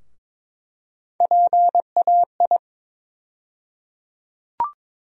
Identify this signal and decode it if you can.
Morse code, heard as a keyed tone, about 22 wpm, 700 Hz. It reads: PAI
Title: USA Call Sign Postfixes - 3 Letters 22wpm